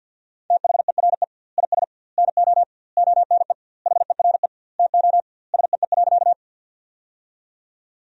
Morse code, heard as a keyed tone, about 50 wpm, 700 Hz. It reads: THERE IS NO ONE HERE TO SEE.
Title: 2x Word-Spacing - Sentences from Top 200 Words - 50wpm